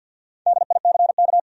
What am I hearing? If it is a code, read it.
Morse code, heard as a keyed tone, about 53 wpm, 700 Hz. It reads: DICK
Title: USA Common Names - Max 4 Letters - 50wpm